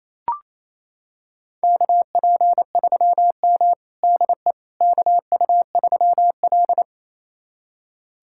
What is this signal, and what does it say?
Morse code, heard as a keyed tone, about 28 wpm, 700 Hz. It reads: KP3M DE KU3L
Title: USA Callsign DE Callsign - 4 Characters - 28wpm